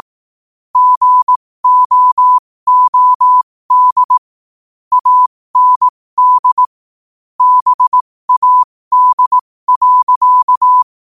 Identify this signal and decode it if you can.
Morse code, heard as a keyed tone, about 18 wpm, spaced as for 15 wpm, 1000 Hz. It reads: GOOD AND BAD.
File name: qotdfiles / 15wpm